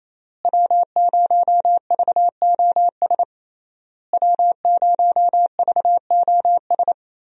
Morse code, heard as a keyed tone, about 28 wpm, 700 Hz. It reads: W0VOS W0VOS
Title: USA Call Signs - 5 Characters - Repeated 3x - 28wpm